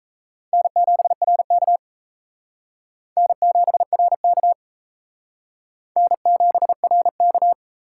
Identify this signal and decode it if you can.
Morse code, heard as a keyed tone, about 34 wpm, 700 Hz. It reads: N7RK N7RK N7RK
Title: Speed Racing - USA Call Signs 28wpm